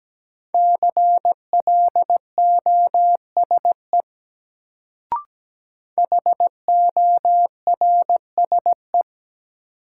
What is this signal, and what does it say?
Morse code, heard as a keyed tone, about 17 wpm, 700 Hz. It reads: CLOSE HORSE
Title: Sentences from Top 300 Words - New Word Drill - 17wpm